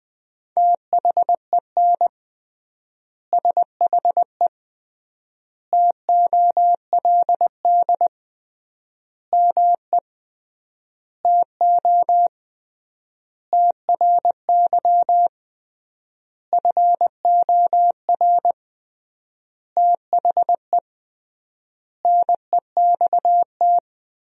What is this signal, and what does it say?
Morse code, heard as a keyed tone, about 20 wpm, 700 Hz. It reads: THEN SHE TOLD ME TO TRY FOR THE NEXT
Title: 3x Word-Spacing - Sentences from Top 400 Words - 20wpm